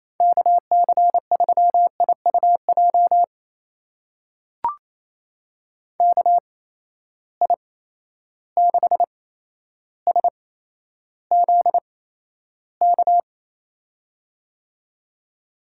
Morse code, heard as a keyed tone, about 28 wpm, spaced as for 12 wpm, 700 Hz. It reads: KC3IUJ KI6SZK
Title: USA Call Signs - 6 Characters - 8x Character Spacing - 28wpm